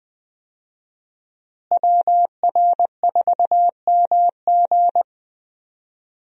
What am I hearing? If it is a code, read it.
Morse code, heard as a keyed tone, about 20 wpm, 700 Hz. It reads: WR4MG
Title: USA Call Signs - 5 Characters - Repeated 3x - 20wpm